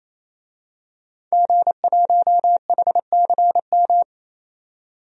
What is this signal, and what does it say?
Morse code, heard as a keyed tone, about 28 wpm, 700 Hz. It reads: G1HCM
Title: DXCC-100 Call Signs - 5 Characters - 28wpm